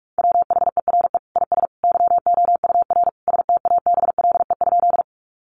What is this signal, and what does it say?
Morse code, heard as a keyed tone, about 44 wpm, 700 Hz. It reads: WHERE IS YOUR STABLE?
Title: Sentences from Top 1500 Words 45wpm